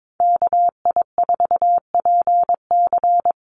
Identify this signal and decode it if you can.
Morse code, heard as a keyed tone, about 22 wpm, 700 Hz. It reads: KI4PC
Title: USA Call Signs Repeated 3x 22wpm